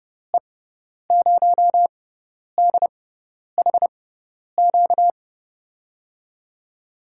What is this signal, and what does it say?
Morse code, heard as a keyed tone, about 30 wpm, spaced as for 10 wpm, 700 Hz. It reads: E0DHQ
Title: USA Call Signs - 6x Character Spacing - 30wpm